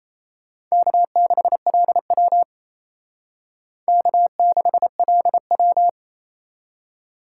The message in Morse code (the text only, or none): K6LW K6LW